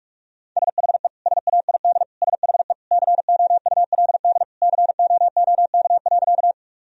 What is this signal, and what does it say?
Morse code, heard as a keyed tone, about 46 wpm, 700 Hz. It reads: SHE SAID SHE COULD COOK.
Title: Sentences from Top 1000 Words - Review Part 4 - 45wpm